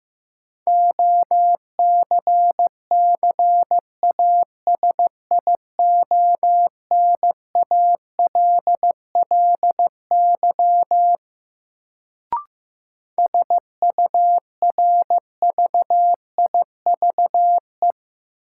Morse code, heard as a keyed tone, about 15 wpm, 700 Hz. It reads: OCCASIONALLY SURVIVE
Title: Sentences from Top 1300 Words - New Word Drill - 15wpm